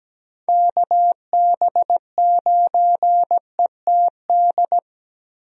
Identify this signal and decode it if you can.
Morse code, heard as a keyed tone, about 17 wpm, 700 Hz. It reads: KB9ETD
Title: USA Call Signs - 6 Characters - Repeated 3x - 17wpm